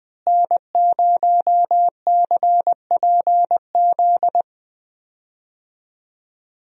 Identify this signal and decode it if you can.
Morse code, heard as a keyed tone, about 21 wpm, 700 Hz. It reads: N0CPZ
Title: USA Call Signs - 5 Characters - Repeated 3x - 20wpm